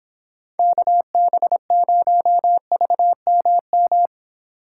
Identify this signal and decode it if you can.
Morse code, heard as a keyed tone, about 28 wpm, 700 Hz. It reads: KB0VMM